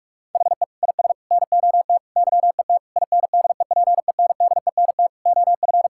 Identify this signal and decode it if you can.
Morse code, heard as a keyed tone, about 43 wpm, 700 Hz. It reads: HE IS NOT YET INDEPENDENT OF
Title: Sentences from Top 1000 Words - Review Part 6 - 45wpm